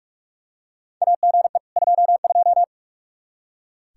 Morse code, heard as a keyed tone, about 42 wpm, 700 Hz. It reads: AGE 22
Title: Practice QSO - Exchange 3 - More Info - 45wpm